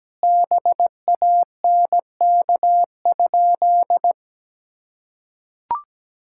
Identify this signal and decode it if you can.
Morse code, heard as a keyed tone, about 17 wpm, 700 Hz. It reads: BANK?
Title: Sentences from Top 1000 Words - Review Part 6 - 17wpm